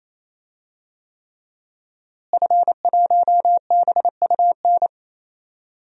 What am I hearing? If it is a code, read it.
Morse code, heard as a keyed tone, about 28 wpm, 700 Hz. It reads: F1BUN
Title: DXCC-100 Call Signs - 5 Characters - Repeated 3x - 28wpm